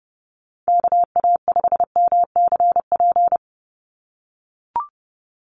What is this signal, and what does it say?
Morse code, heard as a keyed tone, about 30 wpm, 700 Hz. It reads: KA5MCP
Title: USA Call Signs 30wpm